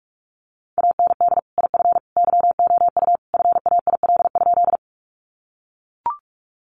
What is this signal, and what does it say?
Morse code, heard as a keyed tone, about 44 wpm, 700 Hz. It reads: AND IF YOU FAIL?
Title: Sentences from Top 1100 Words 45wpm